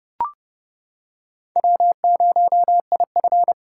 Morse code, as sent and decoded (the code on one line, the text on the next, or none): .-- ----- .. ..-.
W0IF